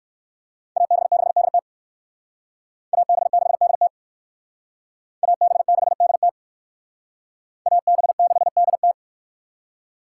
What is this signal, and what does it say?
Morse code, heard as a keyed tone, about 56 wpm, 700 Hz. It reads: AB6DT AB6DT AB6DT AB6DT